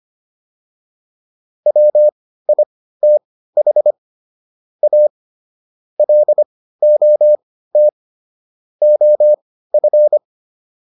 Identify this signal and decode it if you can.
Morse code, heard as a keyed tone, about 25 wpm, spaced as for 15 wpm, 600 Hz. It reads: WITH A LOT OF